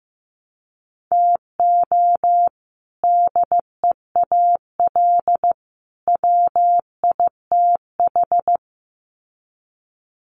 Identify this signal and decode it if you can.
Morse code, heard as a keyed tone, about 15 wpm, 700 Hz. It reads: TO DEAL WITH
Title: Sets of 3 Words 15wpm